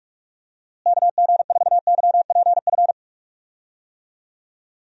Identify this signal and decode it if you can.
Morse code, heard as a keyed tone, about 46 wpm, 700 Hz. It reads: KG4YPF
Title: USA Call Signs - 6 Characters - Repeated 3x - 45wpm